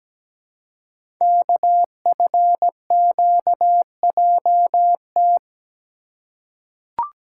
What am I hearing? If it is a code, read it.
Morse code, heard as a keyed tone, about 17 wpm, 700 Hz. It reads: KFQJT
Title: Code Groups - 5 Letters - 17wpm